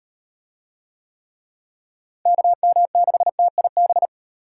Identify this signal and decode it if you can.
Morse code, heard as a keyed tone, about 38 wpm, 700 Hz. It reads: KM6TIB